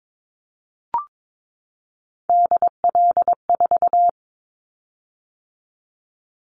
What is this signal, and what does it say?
Morse code, heard as a keyed tone, about 22 wpm, 700 Hz. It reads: DL4
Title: Callsign Prefixes for Top 25 DXCC - 22wpm